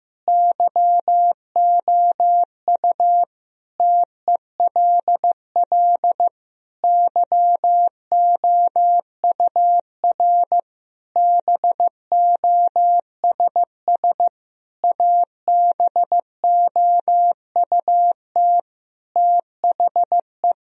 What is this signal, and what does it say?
Morse code, heard as a keyed tone, about 15 wpm, 700 Hz. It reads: YOU TELL YOUR BOSS ABOUT THE
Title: Sentences from Top 1000 Words - Review Part 4 - 15wpm